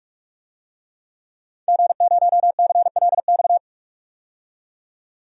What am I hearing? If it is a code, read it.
Morse code, heard as a keyed tone, about 46 wpm, 700 Hz. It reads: G0CLX
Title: DXCC-100 Call Signs - 5 Characters - 45wpm